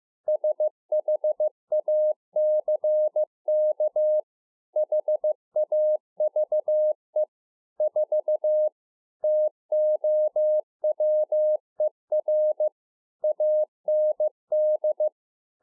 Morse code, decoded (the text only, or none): SHACK HAVE 4 TOWER AND